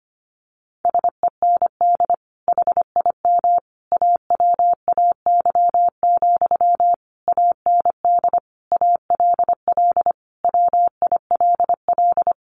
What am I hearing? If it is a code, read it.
Morse code, heard as a keyed tone, about 25 wpm, 700 Hz. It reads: SEND HIM AWAY, AND ALL WILL
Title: Sentences from Top 1500 Words 25wpm